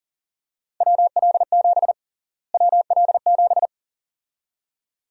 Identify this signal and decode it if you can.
Morse code, heard as a keyed tone, about 37 wpm, 700 Hz. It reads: WL7 WL7
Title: USA Call Sign Prefixes - Repeated 2x - 40wpm